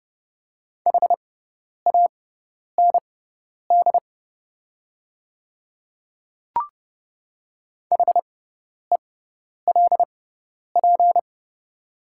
Morse code, heard as a keyed tone, about 30 wpm, spaced as for 10 wpm, 690 Hz. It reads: HAND HELP
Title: 6x Character Spacing - Top 200 Words - 30wpm